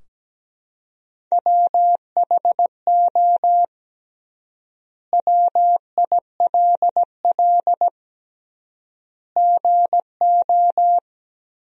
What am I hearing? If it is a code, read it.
Morse code, heard as a keyed tone, about 17 wpm, 700 Hz. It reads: WHO WILL GO
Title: 3x Word-Spacing - Sentences from Top 100 Words - 17wpm